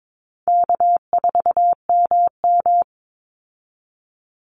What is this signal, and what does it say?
Morse code, heard as a keyed tone, about 22 wpm, 700 Hz. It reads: K4MM
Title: Easy Call Signs Repeated 3x - 22wpm